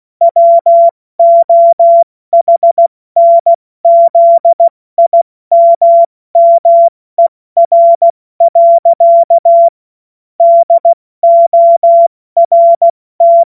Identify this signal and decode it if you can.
Morse code, heard as a keyed tone, about 16 wpm, 680 Hz. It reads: WOHNZIMMER. DORT